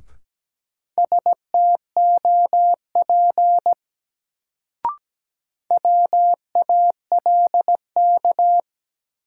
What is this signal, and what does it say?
Morse code, heard as a keyed tone, about 17 wpm, 700 Hz. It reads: STOP WALK
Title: Sentences from Top 300 Words - New Word Drill - 17wpm